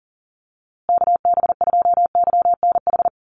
Morse code, heard as a keyed tone, about 40 wpm, 700 Hz. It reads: KB2YNH
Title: USA Call Signs Repeated 3x 40wpm